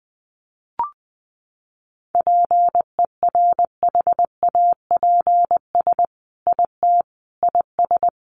Morse code, heard as a keyed tone, about 20 wpm, 700 Hz. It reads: PERHAPS IT IS
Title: Sentences from Top 800 Words 20wpm